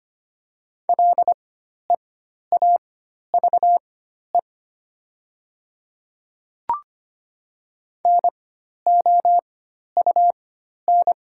Morse code, heard as a keyed tone, about 25 wpm, spaced as for 12 wpm, 700 Hz. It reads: LEAVE NOUN
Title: 4x Character Spacing - Top 500 Words - 25wpm